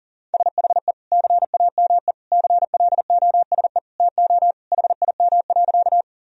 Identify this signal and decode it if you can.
Morse code, heard as a keyed tone, about 40 wpm, 700 Hz. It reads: SHE CAME CLOSE TO HIM.
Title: Sentences from Top 1000 Words - Review Part 1 - 40wpm